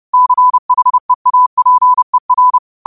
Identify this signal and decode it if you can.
Morse code, heard as a keyed tone, about 30 wpm, 1000 Hz. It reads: CHEAPER